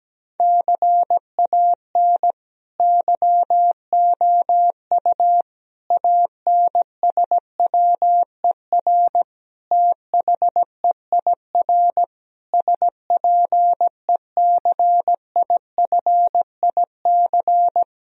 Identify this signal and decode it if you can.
Morse code, heard as a keyed tone, about 17 wpm, 700 Hz. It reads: CAN YOU ANSWER THEIR SPECIFIC
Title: Sentences from Top 1000 Words - Review Part 3 - 17wpm